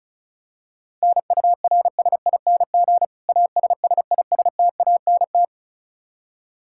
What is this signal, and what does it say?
Morse code, heard as a keyed tone, about 36 wpm, 700 Hz. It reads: NURSING ASSISTANT
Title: Practice QSO - Exchange 3 - More Info - 35wpm